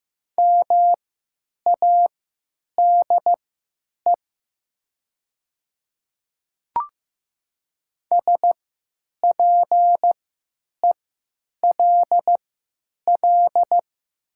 Morse code, heard as a keyed tone, about 15 wpm, spaced as for 9 wpm, 700 Hz. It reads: MADE SPELL
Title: 3x Character Spacing - Top 300 Words - 15wpm